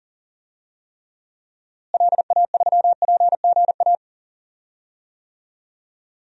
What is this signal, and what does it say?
Morse code, heard as a keyed tone, about 37 wpm, 700 Hz. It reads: LA3PGA
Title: DXCC-100 Call Signs - 6 Characters - Repeated 3x - 40wpm